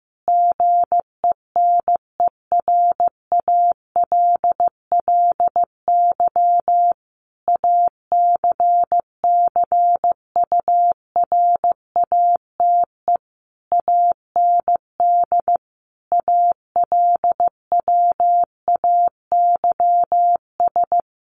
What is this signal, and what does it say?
Morse code, heard as a keyed tone, about 15 wpm, 700 Hz. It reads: GENERALLY ACCURATE AND ALWAYS
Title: Sentences from Top 1200 Words 15wpm